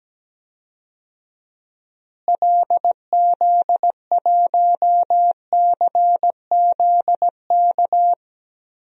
Morse code, heard as a keyed tone, about 17 wpm, 700 Hz. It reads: LZ1CZK
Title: DXCC-100 Call Signs - 6 Characters - Repeated 3x - 17wpm